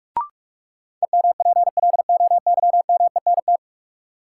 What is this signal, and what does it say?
Morse code, heard as a keyed tone, about 43 wpm, 700 Hz. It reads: EMPLOYMENT
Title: Sentences from Top 1500 Words - New Word Drill - 45wpm